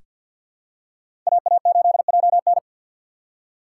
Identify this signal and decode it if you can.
Morse code, heard as a keyed tone, about 48 wpm, 700 Hz. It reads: AA8JN
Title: USA Call Signs - 2x Character Spacing - 50wpm